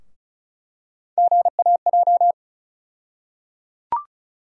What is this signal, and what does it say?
Morse code, heard as a keyed tone, about 36 wpm, 700 Hz. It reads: GAJ
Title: USA Call Sign Postfixes - 3 Letters 35wpm